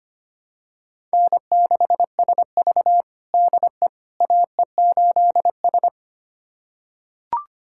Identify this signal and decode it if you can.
Morse code, heard as a keyed tone, about 25 wpm, 700 Hz. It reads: N6SV DE AE8S
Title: USA Callsign DE Callsign - 4 Characters - 25wpm